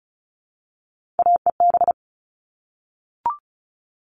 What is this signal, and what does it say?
Morse code, heard as a keyed tone, about 34 wpm, 700 Hz. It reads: AEB
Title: All Cut Numbers - 3-Digit 35wpm